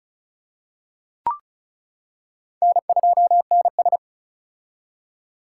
Morse code, heard as a keyed tone, about 36 wpm, 700 Hz. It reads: N2NS
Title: K1USN SST QSO Practice - 35wpm